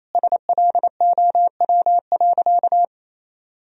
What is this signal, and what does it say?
Morse code, heard as a keyed tone, about 28 wpm, 700 Hz. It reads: SLOW.
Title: Sentences from Top 1000 Words - Review Part 2 - 28wpm